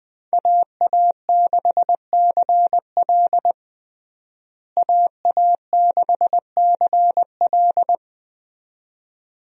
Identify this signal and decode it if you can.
Morse code, heard as a keyed tone, about 20 wpm, 700 Hz. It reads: AA6CL AA6CL
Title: USA Call Signs - 5 Characters - Repeated 2x - 20wpm